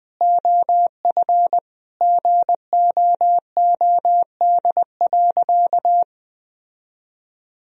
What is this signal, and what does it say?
Morse code, heard as a keyed tone, about 20 wpm, 700 Hz. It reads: OF GOOD.
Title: Sentences from Top 1000 Words - Review Part 3 - 20wpm